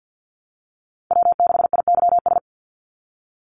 Morse code, heard as a keyed tone, about 50 wpm, 700 Hz. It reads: W6IYS
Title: Call Signs Repeated 2x - 50wpm